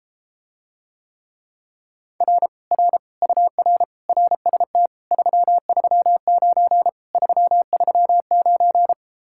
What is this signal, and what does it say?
Morse code, heard as a keyed tone, about 33 wpm, 700 Hz. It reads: R R UR RST 339 339